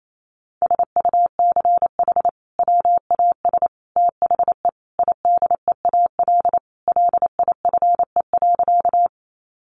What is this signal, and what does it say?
Morse code, heard as a keyed tone, about 28 wpm, 700 Hz. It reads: SUCH WAS THE IDEAL LIFE.
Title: Sentences from Top 1400 Words 28wpm